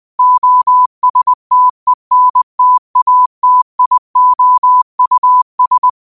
Morse code, heard as a keyed tone, about 20 wpm, 1000 Hz. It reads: OSTENTATIOUS
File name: 20wpm